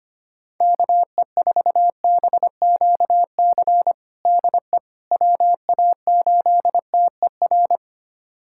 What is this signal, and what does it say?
Morse code, heard as a keyed tone, about 25 wpm, 700 Hz. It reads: KE4BQC DE WA8TER
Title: USA Callsign DE Callsign 25wpm